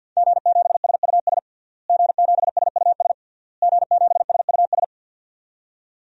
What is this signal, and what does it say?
Morse code, heard as a keyed tone, about 46 wpm, 700 Hz. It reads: G7SUS G7SUS G7SUS
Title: DXCC-100 Call Signs - 5 Characters - Repeated 3x - 50wpm